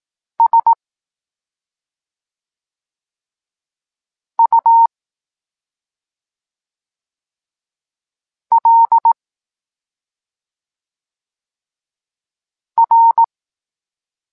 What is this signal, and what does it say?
Morse code, heard as a keyed tone, about 18 wpm, 920 Hz. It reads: S U L R